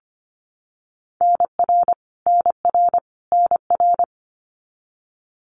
Morse code, heard as a keyed tone, about 25 wpm, 700 Hz. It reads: NR NR NR
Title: Abbreviations Repeated 3x - 25wpm